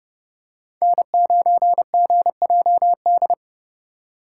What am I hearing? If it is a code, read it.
Morse code, heard as a keyed tone, about 32 wpm, 700 Hz. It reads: N9GJD
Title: USA Call Signs - 5 Characters - 6x Character Spacing - 30wpm